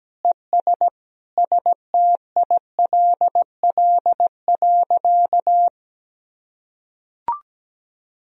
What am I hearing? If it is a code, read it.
Morse code, heard as a keyed tone, about 17 wpm, 700 Hz. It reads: ES STILL.
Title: Sentences from Top 500 Words and Abbreviations 17wpm